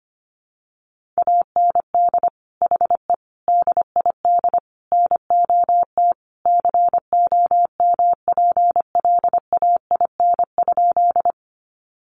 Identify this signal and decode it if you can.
Morse code, heard as a keyed tone, about 25 wpm, 700 Hz. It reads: AND HE DID NOT COMPLAIN?
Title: Sentences from Top 1300 Words 25wpm